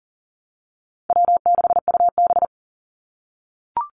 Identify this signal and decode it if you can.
Morse code, heard as a keyed tone, about 40 wpm, 700 Hz. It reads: W6UB
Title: NA SOTA Activator Callsigns 40wpm